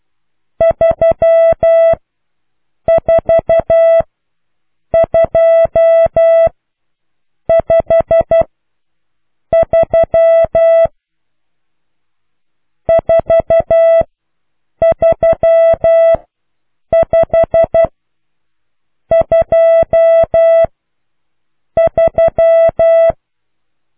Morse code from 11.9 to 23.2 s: 43523